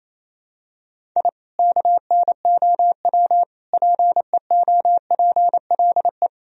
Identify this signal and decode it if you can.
Morse code, heard as a keyed tone, about 28 wpm, 700 Hz. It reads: I KNOW PEOPLE
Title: Sentences from Top 1000 Words - Review Part 4 - 28wpm